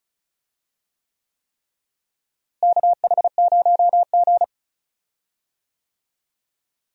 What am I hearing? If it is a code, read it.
Morse code, heard as a keyed tone, about 36 wpm, 700 Hz. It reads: KH0G
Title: USA Call Signs - 4 Characters - Repeated 3x - 35wpm